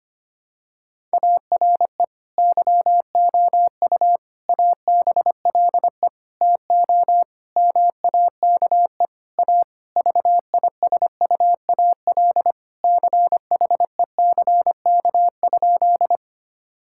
Morse code, heard as a keyed tone, about 25 wpm, 700 Hz. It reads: ARE YOU ABLE TO MAKE A VISUAL CHECK?
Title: Sentences from Top 1000 Words - Review Part 6 - 25wpm